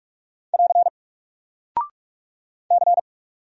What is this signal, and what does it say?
Morse code, heard as a keyed tone, about 44 wpm, 700 Hz. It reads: <AR> C
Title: Single Character (Letter:Number:Punctuation:Prosign) 45wpm